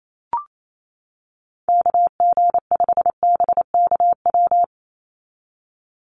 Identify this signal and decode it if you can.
Morse code, heard as a keyed tone, about 28 wpm, 700 Hz. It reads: KG5BKW
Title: USA Call Signs 28wpm